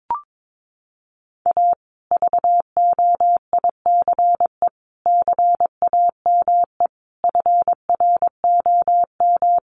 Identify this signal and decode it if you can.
Morse code, heard as a keyed tone, about 22 wpm, 700 Hz. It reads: A VOICE CAME FROM
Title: Sentences from Top 1200 Words 22wpm